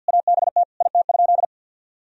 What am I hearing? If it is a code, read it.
Morse code, heard as a keyed tone, about 50 wpm, 700 Hz. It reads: ABT IT?
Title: Sentences from Top 200 Words and Abbreviations 50wpm